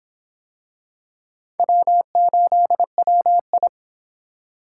Morse code, heard as a keyed tone, about 24 wpm, 700 Hz. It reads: W8WI